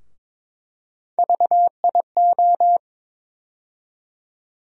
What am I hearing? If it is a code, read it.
Morse code, heard as a keyed tone, about 22 wpm, 700 Hz. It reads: VIO
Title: USA Call Sign Postfixes - 3 Letters 22wpm